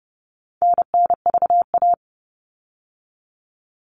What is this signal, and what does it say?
Morse code, heard as a keyed tone, about 30 wpm, 700 Hz. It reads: NNVA
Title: All Cut Numbers - 4-Digit 30wpm